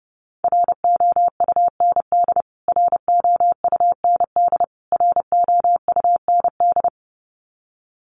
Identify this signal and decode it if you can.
Morse code, heard as a keyed tone, about 29 wpm, 700 Hz. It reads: ROUND ROUND ROUND
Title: Top 200 Words Repeated 30wpm